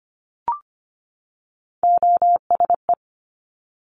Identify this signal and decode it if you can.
Morse code, heard as a keyed tone, about 25 wpm, 700 Hz. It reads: OSE
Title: Top Three Letter Combinations in Words 25wpm